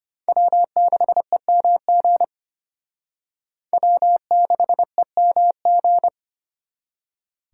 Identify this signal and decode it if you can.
Morse code, heard as a keyed tone, about 34 wpm, 700 Hz. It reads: W6EMG W6EMG